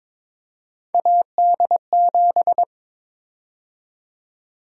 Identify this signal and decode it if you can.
Morse code, heard as a keyed tone, about 22 wpm, 700 Hz. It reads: AD7
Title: USA Call Sign Prefixes - 22wpm